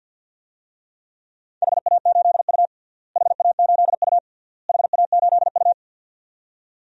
Morse code, heard as a keyed tone, about 47 wpm, 700 Hz. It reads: HA8U HA8U HA8U
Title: DXCC-100 Call Signs - 4 Characters - Repeated 3x - 50wpm